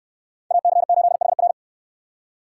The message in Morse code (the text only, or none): AC8HR